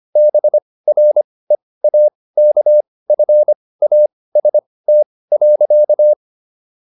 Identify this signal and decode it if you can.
Morse code, heard as a keyed tone, about 25 wpm, spaced as for 18 wpm, 600 Hz. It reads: BREAKFAST.